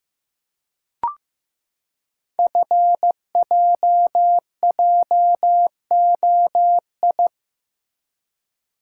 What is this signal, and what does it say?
Morse code, heard as a keyed tone, about 15 wpm, 700 Hz. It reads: FJJOI
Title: Code Groups - 5 Letters - 15wpm